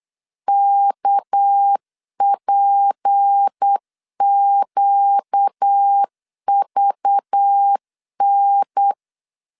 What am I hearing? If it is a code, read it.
Morse code, heard as a keyed tone, about 8 wpm, 800 Hz. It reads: KPQVN
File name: 8wpm